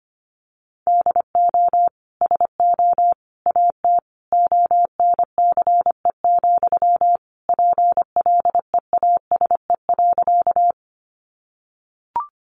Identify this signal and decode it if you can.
Morse code, heard as a keyed tone, about 25 wpm, 700 Hz. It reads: DO SO AT ONCE, PLEASE.
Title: Sentences from Top 1500 Words 25wpm